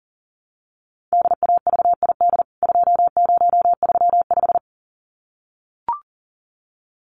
Morse code, heard as a keyed tone, about 40 wpm, 700 Hz. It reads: DAVID 2035
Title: CWT Exchange 40wpm